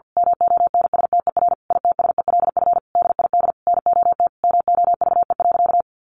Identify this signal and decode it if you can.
Morse code, heard as a keyed tone, about 50 wpm, 700 Hz. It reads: MONSTER ITSELF DID NOT MOVE.
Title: A Sound Of Thunder 50wpm